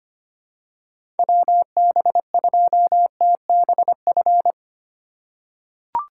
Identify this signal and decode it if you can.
Morse code, heard as a keyed tone, about 25 wpm, 700 Hz. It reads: WB2TBF
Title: USA Call Signs - 6 Characters - 3x Character Spacing - 25wpm